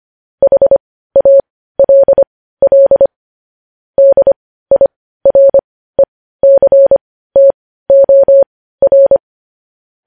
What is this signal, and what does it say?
Morse code, heard as a keyed tone, about 25 wpm, spaced as for 15 wpm, 550 Hz. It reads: HALL DIRECTOR
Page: News Headlines in Morse Code at 15 WPM